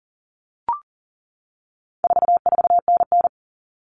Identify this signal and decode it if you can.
Morse code, heard as a keyed tone, about 40 wpm, 700 Hz. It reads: 44NN
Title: Common Cut Numbers - 4-Digit 40wpm